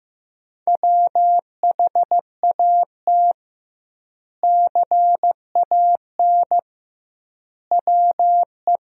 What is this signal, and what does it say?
Morse code, heard as a keyed tone, about 15 wpm, 700 Hz. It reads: WHAT CAN WE
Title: 2x Word-Spacing - Sentences from Top 100 Words - 15wpm